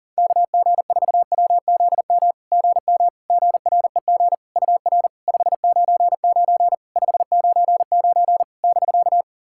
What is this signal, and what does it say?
Morse code, heard as a keyed tone, about 40 wpm, 700 Hz. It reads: KG4WZM GM GREG UR 599 599 <BK>
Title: SOTA QSO Practice 40wpm